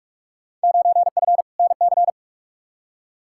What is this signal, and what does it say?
Morse code, heard as a keyed tone, about 46 wpm, 700 Hz. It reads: 9F NC
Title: Field Day Exchanges Encore - 45wpm